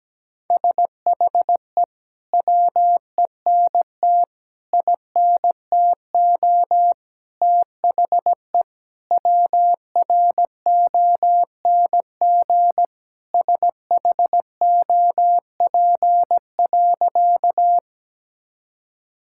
Morse code, heard as a keyed tone, about 17 wpm, 700 Hz. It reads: SHE WENT INTO THE WRONG SHOP.
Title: Sentences from Top 1600 Words 17wpm